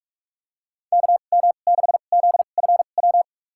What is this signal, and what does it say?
Morse code, heard as a keyed tone, about 46 wpm, spaced as for 33 wpm, 700 Hz. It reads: KM6ZFW
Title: USA Call Signs - 6 Characters - 2x Character Spacing - 45wpm